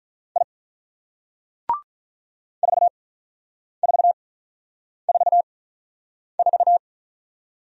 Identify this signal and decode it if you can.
Morse code, heard as a keyed tone, about 45 wpm, spaced as for 9 wpm, 700 Hz. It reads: I 4444